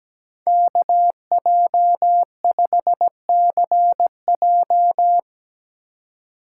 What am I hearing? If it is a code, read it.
Morse code, heard as a keyed tone, about 20 wpm, 700 Hz. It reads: KJ5CJ
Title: Speed Racing - USA Call Signs 17wpm